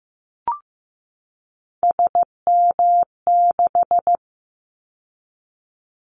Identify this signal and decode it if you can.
Morse code, heard as a keyed tone, about 15 wpm, 700 Hz. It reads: SM6
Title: Callsign Prefixes for Top 25 DXCC - 15wpm